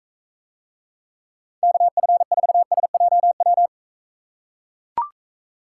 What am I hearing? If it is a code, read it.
Morse code, heard as a keyed tone, about 43 wpm, 700 Hz. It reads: KF4SJW